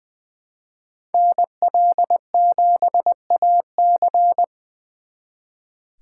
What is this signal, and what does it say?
Morse code, heard as a keyed tone, about 23 wpm, 700 Hz. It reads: NL7AC
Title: Speed Racing - USA 2x2 Call Signs 20wpm